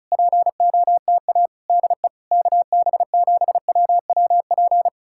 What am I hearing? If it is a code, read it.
Morse code, heard as a keyed tone, about 40 wpm, 700 Hz. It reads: POTA DE KB7WWP